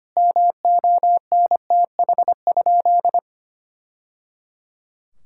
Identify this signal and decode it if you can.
Morse code, heard as a keyed tone, about 25 wpm, 700 Hz. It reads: MONTH?
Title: Sentences from Top 1000 Words - Review Part 6 - 25wpm